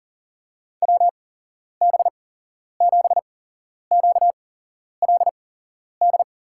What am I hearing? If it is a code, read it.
Morse code, heard as a keyed tone, about 40 wpm, spaced as for 11 wpm, 700 Hz. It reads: WB7QLD
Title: USA Call Signs - 6 Characters - 8x Character Spacing - 40wpm